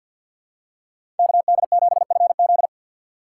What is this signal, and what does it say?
Morse code, heard as a keyed tone, about 52 wpm, 700 Hz. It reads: KD7FZ
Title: USA Call Signs - 5 Characters - 2x Character Spacing - 50wpm